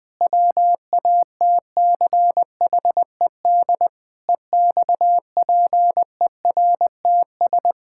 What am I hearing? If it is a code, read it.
Morse code, heard as a keyed tone, about 19 wpm, 700 Hz. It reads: WATCHED EXPERTS
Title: Pangram Challenge v2 - 20wpm